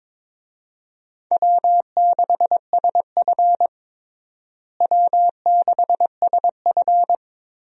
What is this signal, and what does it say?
Morse code, heard as a keyed tone, about 22 wpm, 700 Hz. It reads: W6SF W6SF
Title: USA Call Signs - 4 Characters - Repeated 3x - 22wpm